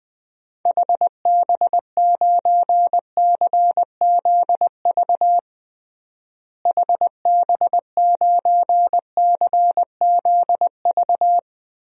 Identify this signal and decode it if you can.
Morse code, heard as a keyed tone, about 21 wpm, 700 Hz. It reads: HB9CZV HB9CZV
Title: DXCC-100 Call Signs - 6 Characters - Repeated 3x - 20wpm